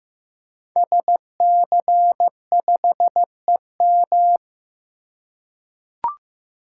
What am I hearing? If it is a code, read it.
Morse code, heard as a keyed tone, about 15 wpm, 700 Hz. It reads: SC5EM
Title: Code Groups - 5 Letter:Numbers - 15wpm